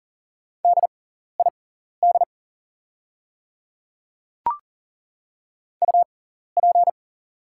Morse code, heard as a keyed tone, about 42 wpm, spaced as for 14 wpm, 690 Hz. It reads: DID UP
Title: 6x Character Spacing - Top 100 Words - 40wpm